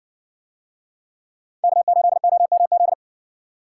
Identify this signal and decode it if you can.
Morse code, heard as a keyed tone, about 60 wpm, 700 Hz. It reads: K8OMZ